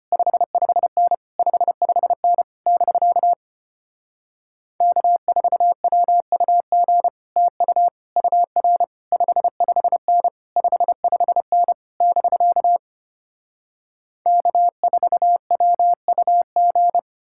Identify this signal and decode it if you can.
Morse code, heard as a keyed tone, about 33 wpm, 700 Hz. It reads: 55N 55N <BK> K4WUG TU UR 55N 55N <BK> K4WUG